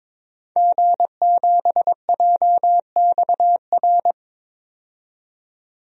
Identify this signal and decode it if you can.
Morse code, heard as a keyed tone, about 22 wpm, 700 Hz. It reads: G7JXR
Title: DXCC-100 Call Signs - 5 Characters - 22wpm